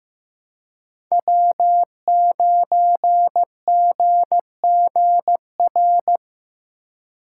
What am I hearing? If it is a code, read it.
Morse code, heard as a keyed tone, about 15 wpm, 700 Hz. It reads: W9GGR
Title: USA Call Signs - 5 Characters - 4x Character Spacing - 15wpm